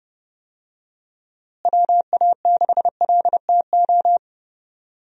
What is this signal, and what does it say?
Morse code, heard as a keyed tone, about 30 wpm, 700 Hz. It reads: WA6LTO
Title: USA Call Signs - 6 Characters - Repeated 2x - 30wpm